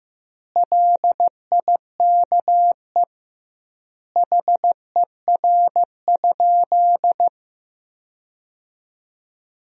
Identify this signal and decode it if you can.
Morse code, heard as a keyed tone, about 15 wpm, 700 Hz. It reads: LIKE HER?
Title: 2x Word-Spacing - Sentences from Top 100 Words - 15wpm